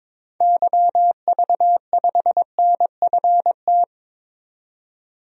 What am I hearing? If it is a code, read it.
Morse code, heard as a keyed tone, about 22 wpm, 700 Hz. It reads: YV5NFT
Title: DXCC-100 Call Signs - 6 Characters - 22wpm